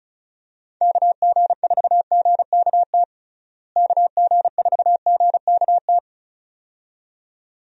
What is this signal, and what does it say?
Morse code, heard as a keyed tone, about 34 wpm, 700 Hz. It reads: KG4GKT KG4GKT
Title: USA Call Signs - 6 Characters - Repeated 2x - 35wpm